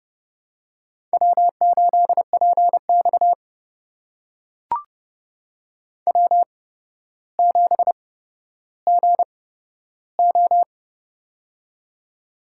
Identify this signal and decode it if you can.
Morse code, heard as a keyed tone, about 31 wpm, spaced as for 13 wpm, 700 Hz. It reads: W8PX W7GO
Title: USA Call Signs - 4 Characters - 8x Character Spacing - 30wpm